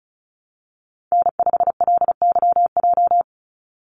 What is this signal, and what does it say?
Morse code, heard as a keyed tone, about 34 wpm, 700 Hz. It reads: N5LYJ
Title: Call Signs Repeated 2x - 35wpm